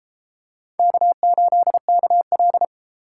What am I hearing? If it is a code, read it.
Morse code, heard as a keyed tone, about 33 wpm, 700 Hz. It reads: K8KL